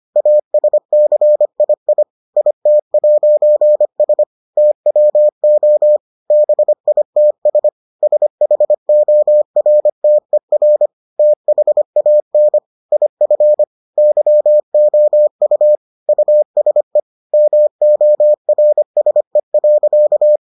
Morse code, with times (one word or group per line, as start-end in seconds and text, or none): 0.2-2.0 ASCII
2.4-4.2 IT'S
4.6-6.0 TWO
6.3-7.7 BITS
8.0-10.9 SHORTER
11.2-12.6 THAN
12.9-13.6 IF
14.0-15.8 YOU
16.1-17.0 USE
17.3-20.4 MORSE.